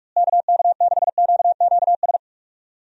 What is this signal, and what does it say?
Morse code, heard as a keyed tone, about 41 wpm, 700 Hz. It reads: KK6QQS
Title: USA Call Signs - 6 Characters - Repeated 2x - 45wpm